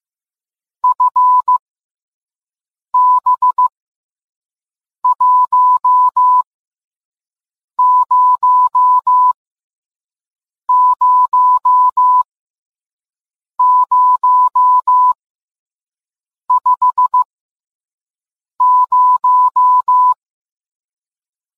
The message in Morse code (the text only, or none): FB100050